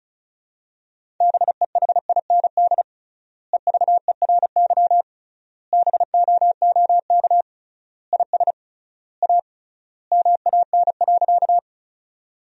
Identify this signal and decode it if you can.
Morse code, heard as a keyed tone, about 36 wpm, 700 Hz. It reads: BEHIND EVERY BOOK IS A MAN.
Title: 3x Word-Spacing - Sentences from Top 500 Words - 35wpm